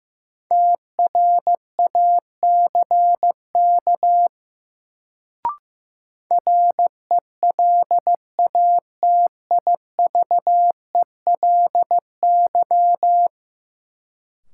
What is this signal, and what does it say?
Morse code, heard as a keyed tone, about 16 wpm, 700 Hz. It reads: TRACK RELATIVELY
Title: Sentences from Top 1100 Words - New Word Drill - 15wpm